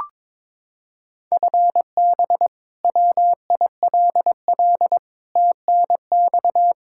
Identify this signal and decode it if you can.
Morse code, heard as a keyed tone, about 22 wpm, 700 Hz. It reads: FB WILL TNX
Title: Practice QSO - Exchange 3 - More Info - 22wpm